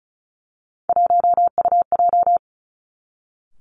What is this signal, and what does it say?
Morse code, heard as a keyed tone, about 36 wpm, 700 Hz. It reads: 1UJ
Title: Three Challenging Letter-Number Combinations 35wpm